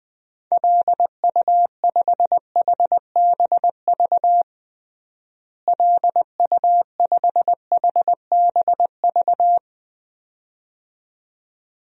Morse code, heard as a keyed tone, about 20 wpm, 700 Hz. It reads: LU5HBV LU5HBV
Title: DXCC-100 Call Signs - 6 Characters - Repeated 2x - 20wpm